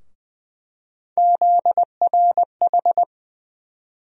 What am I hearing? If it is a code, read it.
Morse code, heard as a keyed tone, about 21 wpm, 700 Hz. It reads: ZRH
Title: USA Call Sign Postfixes - 3 Letters 20wpm